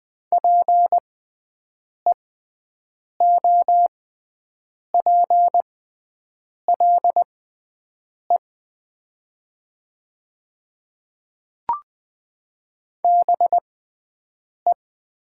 Morse code, heard as a keyed tone, about 20 wpm, spaced as for 7 wpm, 700 Hz. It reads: PEOPLE BE
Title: 6x Character Spacing - Top 100 Words - 20wpm